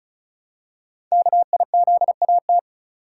Morse code, heard as a keyed tone, about 33 wpm, 700 Hz. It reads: KIZAT
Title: Code Groups - 5 Letters - 35wpm